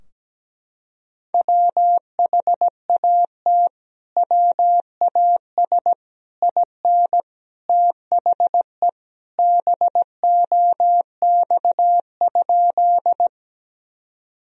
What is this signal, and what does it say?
Morse code, heard as a keyed tone, about 17 wpm, 700 Hz. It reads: WHAT WAS IN THE BOX?
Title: Sentences from Top 1000 Words - Review Part 2 - 17wpm